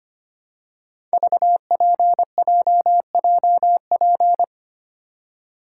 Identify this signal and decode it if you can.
Morse code, heard as a keyed tone, about 25 wpm, 700 Hz. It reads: VPJJP
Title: Code Groups - 5 Letters - 25wpm